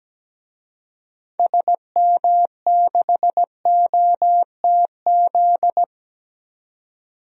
Morse code, heard as a keyed tone, about 17 wpm, 700 Hz. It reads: SM6OTZ
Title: DXCC-100 Call Signs - 6 Characters - Repeated 2x - 17wpm